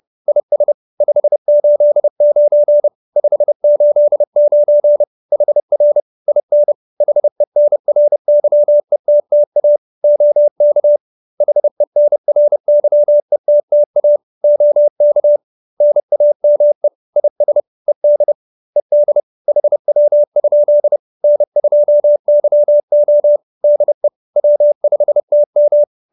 Morse code, heard as a keyed tone, about 30 wpm, 590 Hz. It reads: IS 589 589 HR IN HENRYETTA OK HENRYETTA OK NAME IS ED ED HW? N2YO DE W5TM